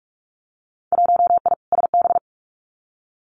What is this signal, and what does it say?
Morse code, heard as a keyed tone, about 46 wpm, 700 Hz. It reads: 1I SB
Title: Winter Field Day Exchanges Encore 45wpm